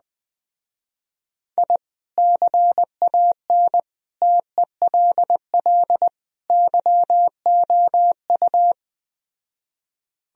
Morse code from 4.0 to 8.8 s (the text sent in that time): TELL YOU